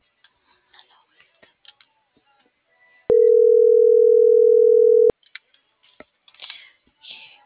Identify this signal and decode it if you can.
Ringback tone (North America), 440 + 480 Hz.